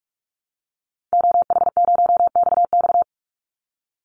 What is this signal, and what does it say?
Morse code, heard as a keyed tone, about 44 wpm, 700 Hz. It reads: OH0XX
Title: Call Signs Repeated 3x - 45wpm